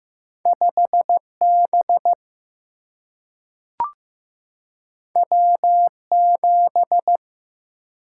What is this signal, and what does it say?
Morse code, heard as a keyed tone, about 15 wpm, 700 Hz. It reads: 5 B W 7
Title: Single-Letter Single-Number 15wpm